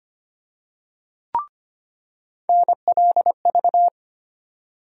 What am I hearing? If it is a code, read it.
Morse code, heard as a keyed tone, about 25 wpm, 700 Hz. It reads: NLV
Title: USA Call Sign Postfixes - 3 Letters 25wpm